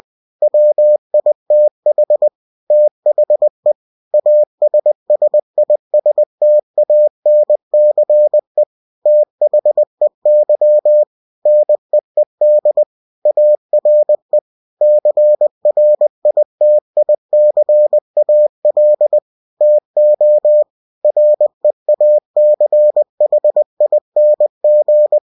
Morse code, heard as a keyed tone, about 20 wpm, 600 Hz. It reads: WITH THE ASSISTANCE THEY NEED ARE CRITICAL TO REACHING
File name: cw-books / winnie-the-pooh / winnie-20wpm-page-0120